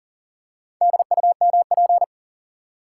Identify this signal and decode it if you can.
Morse code, heard as a keyed tone, about 37 wpm, 700 Hz. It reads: DUMP
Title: Four Letter Words - 40wpm